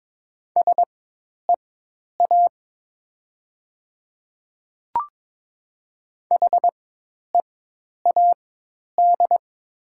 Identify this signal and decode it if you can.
Morse code, heard as a keyed tone, about 22 wpm, spaced as for 10 wpm, 700 Hz. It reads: SEA HEAD
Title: 4x Character Spacing - Top 500 Words - 22wpm